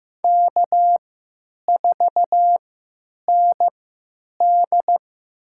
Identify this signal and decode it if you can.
Morse code, heard as a keyed tone, about 15 wpm, spaced as for 9 wpm, 700 Hz. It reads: K4ND
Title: USA Call Signs - 4 Characters - 3x Character Spacing - 15wpm